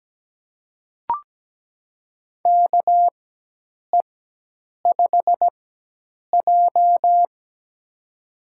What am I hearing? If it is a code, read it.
Morse code, heard as a keyed tone, about 17 wpm, spaced as for 8 wpm, 700 Hz. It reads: KE5J
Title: USA Call Signs - 4x Character Spacing - 17wpm